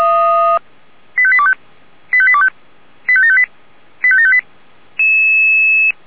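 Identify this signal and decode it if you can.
Five-tone selective call (ZVEI), heard as quick runs of five tones, tones of about 70 ms. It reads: 85825 85825 85758 85758
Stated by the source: ZVEI five-tone call received by SDR.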